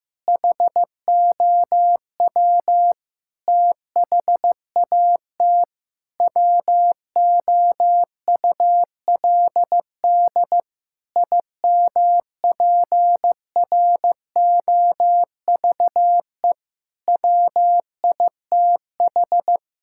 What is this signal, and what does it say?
Morse code, heard as a keyed tone, about 15 wpm, 700 Hz. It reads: HOW THAT WOULD IMPROVE WITH
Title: Sentences from Top 1000 Words - Review Part 5 - 15wpm